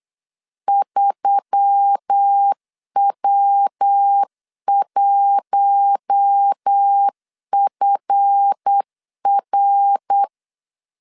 Morse code, heard as a keyed tone, about 8 wpm, 800 Hz. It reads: 3W1FR
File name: LEZIONI / 7 GF9Z / 7 GF9 8wpm 3545